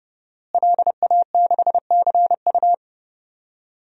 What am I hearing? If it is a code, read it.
Morse code, heard as a keyed tone, about 32 wpm, 700 Hz. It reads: LA6CU
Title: DXCC-100 Call Signs - 5 Characters - Repeated 3x - 30wpm